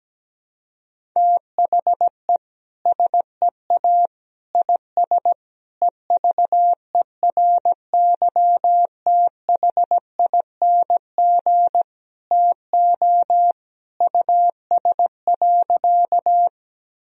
Morse code, heard as a keyed tone, about 17 wpm, 700 Hz. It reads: THE SEA IS EVERYTHING TO US.
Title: Sentences from Top 1000 Words - Review Part 1 - 17wpm